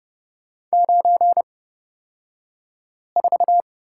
Morse code, heard as a keyed tone, about 30 wpm, spaced as for 5 wpm, 700 Hz. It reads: 94
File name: Lesson 37d - Letter x - Two Characters 30wpm